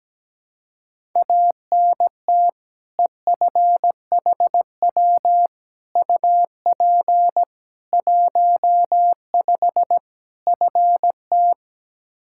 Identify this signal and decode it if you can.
Morse code, heard as a keyed tone, about 17 wpm, 700 Hz. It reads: ANT EFHW UP 15 FT
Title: Practice QSO - Exchange 2 - Info - 17wpm